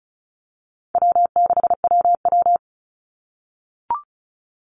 Morse code, heard as a keyed tone, about 34 wpm, 700 Hz. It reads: W6WW
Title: NA SOTA Chaser Callsigns 35wpm